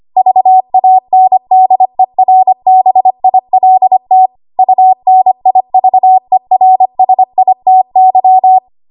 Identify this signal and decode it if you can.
Morse code, heard as a keyed tone, about 25 wpm, 750 Hz. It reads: VANDERBILT UNIVERSITY
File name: Archive / 25 WPM / 25WPM